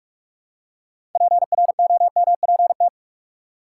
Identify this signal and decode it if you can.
Morse code, heard as a keyed tone, about 45 wpm, 700 Hz. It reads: PROMPT
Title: Six Letter Words - 45wpm